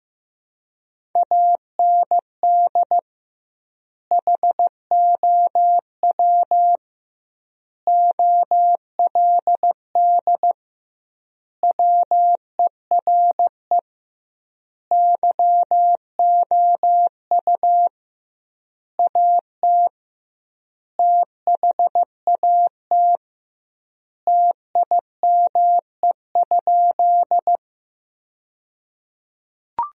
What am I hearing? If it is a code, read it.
Morse code, heard as a keyed tone, about 16 wpm, 700 Hz. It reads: AND HOW OLD WERE YOU AT THAT TIME?
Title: 2x Word-Spacing - Sentences from Top 200 Words - 15wpm